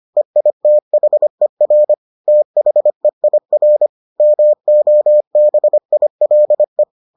Morse code, heard as a keyed tone, about 25 wpm, 600 Hz. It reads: EITHER THEIR MOBILE